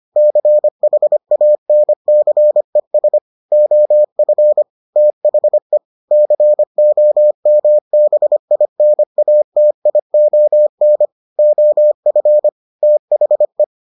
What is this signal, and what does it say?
Morse code, heard as a keyed tone, about 25 wpm, 600 Hz. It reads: CHANCES OF THE COMBINATION OF THE